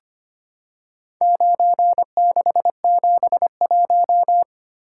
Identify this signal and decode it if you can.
Morse code, heard as a keyed tone, about 25 wpm, 700 Hz. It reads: 9671
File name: practice / 25wpm / Four Letter-Number Combinations - 25wpm